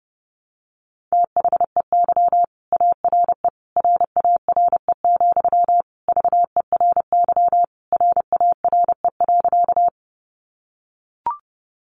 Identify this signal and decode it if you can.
Morse code, heard as a keyed tone, about 28 wpm, 700 Hz. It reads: THEY ARE RARE, VERY RARE.
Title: Sentences from Top 1100 Words 30wpm